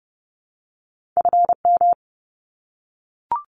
Morse code, heard as a keyed tone, about 30 wpm, 700 Hz. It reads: FM
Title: Abbreviations 30wpm